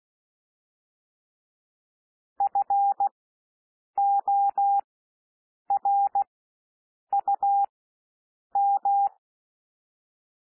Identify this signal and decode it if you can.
Morse code, heard as a keyed tone, about 16 wpm, spaced as for 7 wpm, 800 Hz. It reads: FORUM
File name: SOUND WORDS 16 wpm